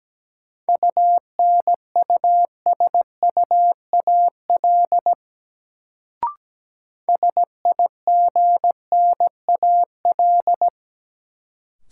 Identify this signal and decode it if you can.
Morse code, heard as a keyed tone, about 17 wpm, 700 Hz. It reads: UNUSUAL SIGNAL
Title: Sentences from Top 1400 Words - New Word Drill Alt - 17wpm